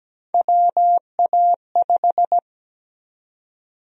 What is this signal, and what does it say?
Morse code, heard as a keyed tone, about 17 wpm, 700 Hz. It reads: WA5
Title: USA Call Sign Prefixes - 17wpm